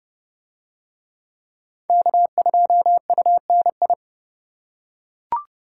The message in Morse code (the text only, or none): K2UNI